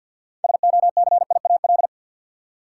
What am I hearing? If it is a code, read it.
Morse code, heard as a keyed tone, about 52 wpm, 700 Hz. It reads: SOCIAL
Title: K1USN SST QSO Practice 50wpm